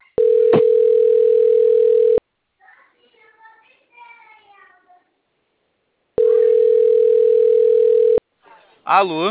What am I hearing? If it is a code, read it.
Ringback tone (North America), 440 + 480 Hz, with speech.
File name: out- - - - - .34097